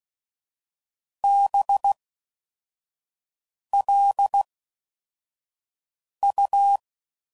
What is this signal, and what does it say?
Morse code, heard as a keyed tone, about 16 wpm, spaced as for 4 wpm, 800 Hz. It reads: BLU